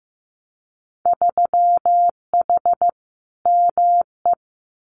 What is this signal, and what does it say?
Morse code, heard as a keyed tone, about 15 wpm, 700 Hz. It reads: 3H ME
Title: Winter Field Day Exchanges Encore 15wpm